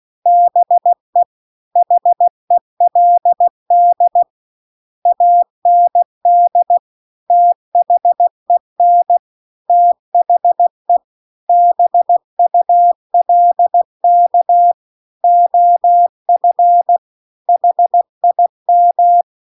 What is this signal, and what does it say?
Morse code, heard as a keyed tone, about 16 wpm, 700 Hz. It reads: BE HELD AND THEN THE BULK OF HIM
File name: CW-The-Big-Twoheaded-River-2-16WPM